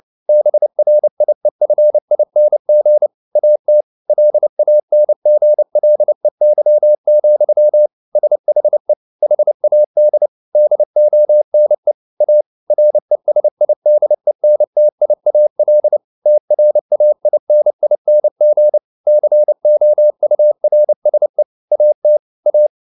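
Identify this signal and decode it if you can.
Morse code, heard as a keyed tone, about 29 wpm, 600 Hz. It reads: BRIEFING AT LANGLEY, SHE HAD DONE A RESIDENTIAL TRAINING COURSE AT A
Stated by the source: Morse code at 29 wpm, sending Ken Follett's novel Never in English, from CW med Gnister.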